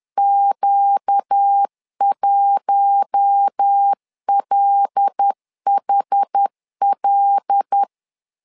Morse code, heard as a keyed tone, about 10 wpm, 800 Hz. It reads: Q1LHL
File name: LEZIONI / 5 LBVC / 5 L 10wpm 2259